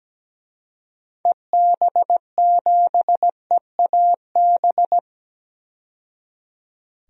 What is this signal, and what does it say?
Morse code, heard as a keyed tone, about 17 wpm, 700 Hz. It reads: EB7EAB
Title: DXCC-100 Call Signs - 6 Characters - 17wpm